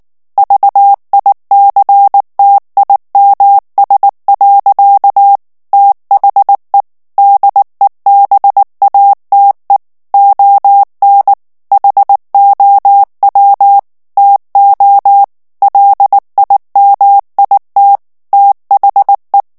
Morse code, heard as a keyed tone, about 19 wpm, 800 Hz. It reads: VICTIMS. THE DEBATE ON HOW TO LIMIT THE
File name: morse mp3 / ANGLAIS texte clair 2 vitesse 18 mots minute